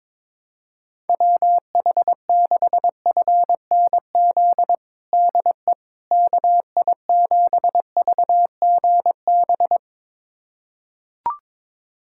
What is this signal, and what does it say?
Morse code, heard as a keyed tone, about 21 wpm, 700 Hz. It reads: WH6FNZ DE KI7VGB
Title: USA Callsign DE Callsign - 6 Characters - 22wpm